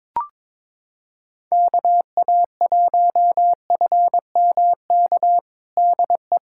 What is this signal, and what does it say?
Morse code, heard as a keyed tone, about 22 wpm, 700 Hz. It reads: KA1FMK DE
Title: USA Callsign DE Callsign - 6 Characters - 22wpm